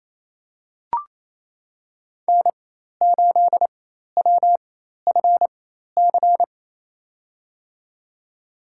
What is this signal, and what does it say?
Morse code, heard as a keyed tone, about 28 wpm, spaced as for 13 wpm, 700 Hz. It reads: N8WFC
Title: USA Call Signs - 5 Characters - 4x Character Spacing - 28wpm